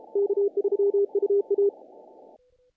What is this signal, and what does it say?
Morse code, heard as a keyed tone, about 36 wpm, 400 Hz. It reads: K3UA
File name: contests / naqp-cw-k5tr-op-k5pi- -1800